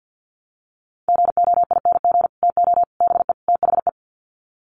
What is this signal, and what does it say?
Morse code, heard as a keyed tone, about 50 wpm, 700 Hz. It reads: GOING TO BE THE
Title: Sets of 4 Words 50wpm